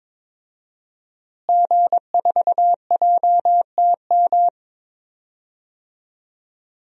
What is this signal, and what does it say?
Morse code, heard as a keyed tone, about 22 wpm, 700 Hz. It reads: G4JTM
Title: DXCC-100 Call Signs - 5 Characters - Repeated 3x - 22wpm